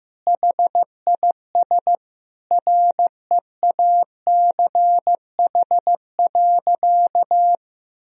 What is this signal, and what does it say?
Morse code, heard as a keyed tone, about 16 wpm, 700 Hz. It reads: HIS REACH.
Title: Sentences from Top 1000 Words - Review Part 2 - 15wpm